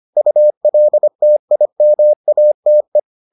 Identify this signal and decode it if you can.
Morse code, heard as a keyed tone, about 25 wpm, 600 Hz. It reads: ULTIMATE